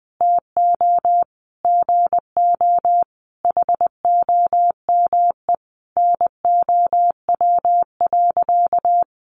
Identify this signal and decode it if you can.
Morse code, heard as a keyed tone, about 20 wpm, 700 Hz. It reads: TO GO HOME NOW.
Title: Sentences from Top 1300 Words 20wpm